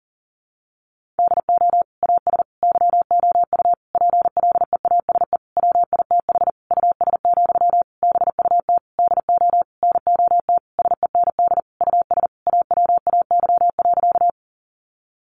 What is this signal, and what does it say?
Morse code, heard as a keyed tone, about 40 wpm, 700 Hz. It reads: DO AS YOU PLEASE WITH US, BUT DO NOT SEND US AWAY.
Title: Sentences from Top 1500 Words 40wpm